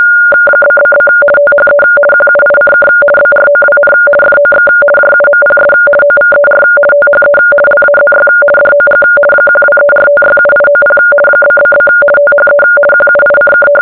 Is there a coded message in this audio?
Yes: RTTY from 0.6 to 13.7 s, about 50 baud. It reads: ??? THE QUICK BROWN FOX JUMPS OVER THE LAZY DOG 1234567890 ??? THE QUICK